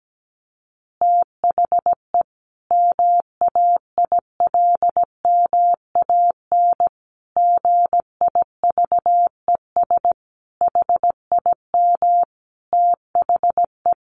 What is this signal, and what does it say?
Morse code, heard as a keyed tone, about 17 wpm, 700 Hz. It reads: THE MAILMAN GIVES HIM THE
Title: Additional Easy Phrases 17wpm